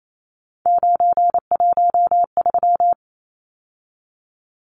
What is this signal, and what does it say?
Morse code, heard as a keyed tone, about 28 wpm, 700 Hz. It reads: 913
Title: Numbers - 3-Digit 28wpm